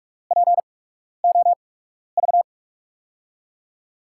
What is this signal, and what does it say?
Morse code, heard as a keyed tone, about 48 wpm, spaced as for 12 wpm, 700 Hz. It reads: POV
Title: USA Call Signs - 8x Character Spacing - 45wpm